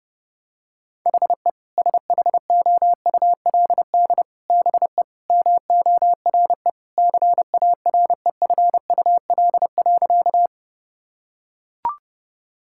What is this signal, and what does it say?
Morse code, heard as a keyed tone, about 30 wpm, 700 Hz. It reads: HE SHOULD BE MORE CAREFUL.
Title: Sentences from Top 1600 Words 30wpm